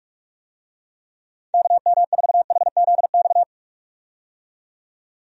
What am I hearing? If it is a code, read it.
Morse code, heard as a keyed tone, about 43 wpm, 700 Hz. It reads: KM4HZX
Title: USA Call Signs - 6 Characters - 45wpm